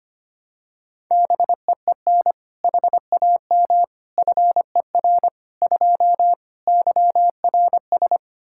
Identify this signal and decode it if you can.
Morse code, heard as a keyed tone, about 25 wpm, 700 Hz. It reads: BEEN HAM FER 2 YRS
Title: Practice QSO - Exchange 3 - More Info - 25wpm